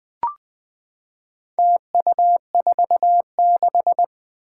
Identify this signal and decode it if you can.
Morse code, heard as a keyed tone, about 20 wpm, 700 Hz. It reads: TU46
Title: Four Challenging Letter-Number Combinations 20wpm